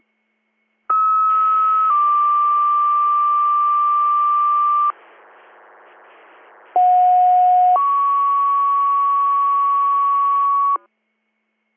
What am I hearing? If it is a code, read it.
Two-tone page, heard as a one-second tone, then a three-second tone, about 1290 then 1180 Hz, about 730 then 1120 Hz. It is heard from 0.9 to 10.8 s.